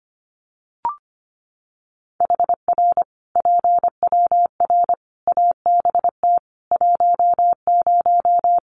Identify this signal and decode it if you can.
Morse code, heard as a keyed tone, about 25 wpm, 700 Hz. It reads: HR PWR ABT 10
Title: Common QSO Phrases 25wpm